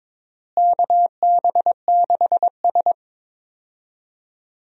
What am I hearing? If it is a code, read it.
Morse code, heard as a keyed tone, about 22 wpm, 700 Hz. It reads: KB6S
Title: USA Call Signs - 4 Characters - Repeated 3x - 22wpm